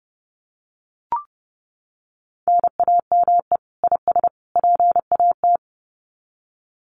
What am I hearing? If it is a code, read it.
Morse code, heard as a keyed tone, about 30 wpm, 700 Hz. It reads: NAME IS PAT
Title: Common QSO Phrases 30wpm